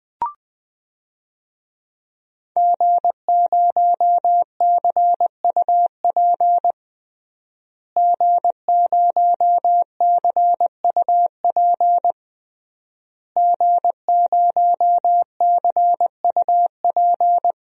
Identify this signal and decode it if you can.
Morse code, heard as a keyed tone, about 20 wpm, 700 Hz. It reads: G0CUP G0CUP G0CUP
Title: DXCC-100 Call Signs - 5 Characters - Repeated 3x - 20wpm